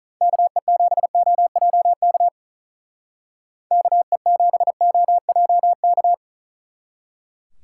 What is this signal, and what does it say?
Morse code, heard as a keyed tone, about 44 wpm, 700 Hz. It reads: KE7OJK KE7OJK
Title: Speed Racing - USA 2x3 Call Signs 35wpm